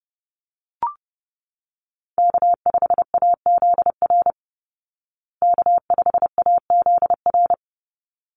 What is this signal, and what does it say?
Morse code, heard as a keyed tone, about 30 wpm, 700 Hz. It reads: K5AZR K5AZR
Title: Call Signs Repeated 3x - 30wpm